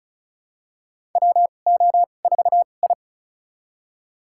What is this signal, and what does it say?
Morse code, heard as a keyed tone, about 36 wpm, spaced as for 25 wpm, 700 Hz. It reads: WO4I
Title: USA Call Signs - 2x Character Spacing - 35wpm